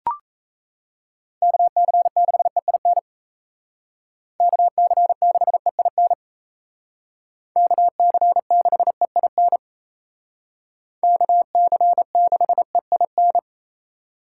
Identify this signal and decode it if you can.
Morse code, heard as a keyed tone, about 33 wpm, 700 Hz. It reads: KC6EIN KC6EIN KC6EIN KC6EIN
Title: Speed Racing - USA 2x3 Call Signs 28wpm